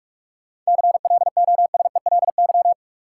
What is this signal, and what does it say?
Morse code, heard as a keyed tone, about 44 wpm, 700 Hz. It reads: CLOSELY
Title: Sentences from Top 1400 Words - New Word Drill Alt - 45wpm